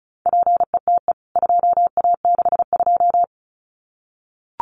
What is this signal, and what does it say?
Morse code, heard as a keyed tone, about 34 wpm, 700 Hz. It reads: PETE 2A62
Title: CWT Exchange 35wpm